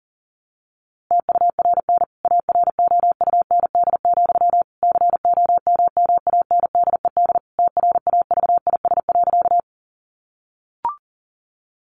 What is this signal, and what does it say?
Morse code, heard as a keyed tone, about 40 wpm, 700 Hz. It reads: TURN AROUND, COMMANDED TRAVIS.
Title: A Sound Of Thunder 40wpm